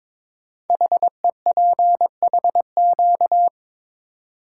Morse code, heard as a keyed tone, about 21 wpm, 700 Hz. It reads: HEPHQ
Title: Code Groups - 5 Letters - 22wpm